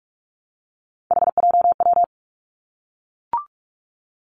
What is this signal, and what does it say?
Morse code, heard as a keyed tone, about 46 wpm, 700 Hz. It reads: HJW
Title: Three Challenging Letter-Number Combinations 45wpm